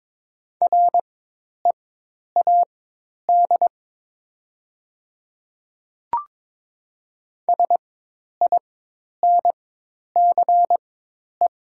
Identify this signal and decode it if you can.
Morse code, heard as a keyed tone, about 22 wpm, spaced as for 10 wpm, 700 Hz. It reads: READ SINCE
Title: 4x Character Spacing - Top 300 Words - 22wpm